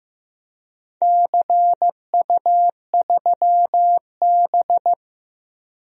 Morse code, heard as a keyed tone, about 15 wpm, 700 Hz. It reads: CU3B
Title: DXCC-100 Call Signs - 4 Characters - 15wpm